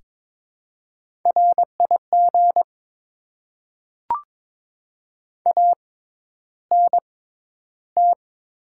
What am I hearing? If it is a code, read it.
Morse code, heard as a keyed tone, about 22 wpm, spaced as for 11 wpm, 700 Hz. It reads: RIG ANT
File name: Top 20 QSO Elements - 6x Character Spacing - 22wpm